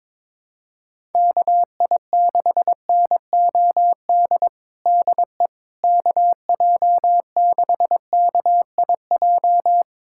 Morse code, heard as a keyed tone, about 22 wpm, 700 Hz. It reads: KI6NOD DE KJ6KIJ
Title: USA Callsign DE Callsign - 6 Characters - 22wpm